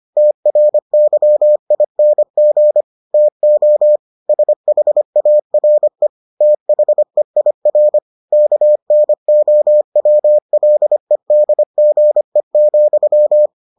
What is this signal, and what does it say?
Morse code, heard as a keyed tone, about 25 wpm, 600 Hz. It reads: TRYING TO SHARE THEIR KNOWLEDGE,